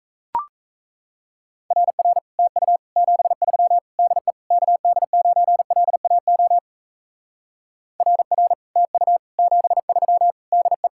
Morse code, heard as a keyed tone, about 36 wpm, 700 Hz. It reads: RR TU 73 DE KD9LAO RR TU 73 DE